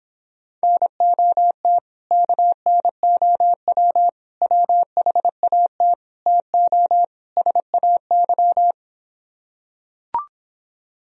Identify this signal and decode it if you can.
Morse code, heard as a keyed tone, about 26 wpm, 700 Hz. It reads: NOT KNOW WHAT TO SAY